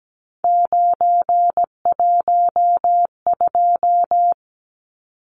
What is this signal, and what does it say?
Morse code, heard as a keyed tone, about 17 wpm, 700 Hz. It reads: 912
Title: Numbers - 3-Digit 17wpm